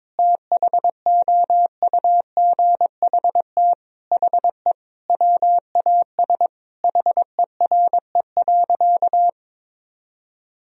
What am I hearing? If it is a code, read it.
Morse code, heard as a keyed tone, about 22 wpm, 700 Hz. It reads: THOUGHT HE WAS HERE.
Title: Sentences from Top 1000 Words - Review Part 3 - 22wpm